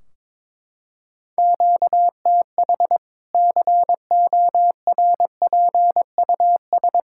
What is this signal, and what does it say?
Morse code, heard as a keyed tone, about 22 wpm, 700 Hz. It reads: QTH CORPUS
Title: K1USN SST QSO Practice 22wpm